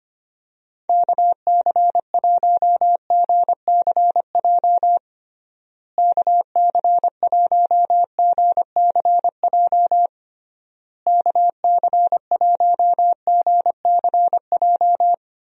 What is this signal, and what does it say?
Morse code, heard as a keyed tone, about 25 wpm, 700 Hz. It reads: KC1GCJ KC1GCJ KC1GCJ
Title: USA Call Signs - 6 Characters - Repeated 3x - 25wpm